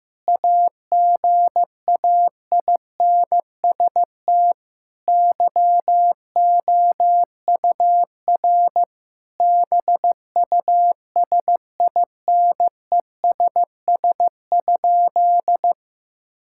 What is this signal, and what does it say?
Morse code, heard as a keyed tone, about 15 wpm, 700 Hz. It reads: AGAINST YOUR BUSINESS?
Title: Sentences from Top 1000 Words - Review Part 5 - 15wpm